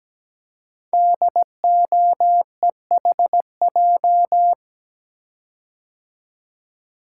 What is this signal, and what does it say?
Morse code, heard as a keyed tone, about 17 wpm, 700 Hz. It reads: DOEHJ
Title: Code Groups - 5 Letters - 17wpm